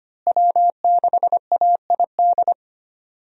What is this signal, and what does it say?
Morse code, heard as a keyed tone, about 25 wpm, 700 Hz. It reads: W6AID
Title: USA Call Signs - 5 Characters - Repeated 3x - 25wpm